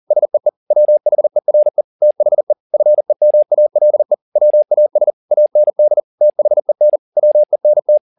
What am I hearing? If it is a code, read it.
Morse code, heard as a keyed tone, about 40 wpm, 600 Hz. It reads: SEE WHERE THE FEMALE WAS AND THEN WENT